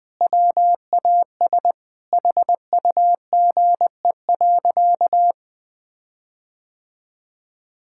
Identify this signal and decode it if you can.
Morse code, heard as a keyed tone, about 20 wpm, 700 Hz. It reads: WAS HUGE.
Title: Sentences with 6-Word Phrases - 20wpm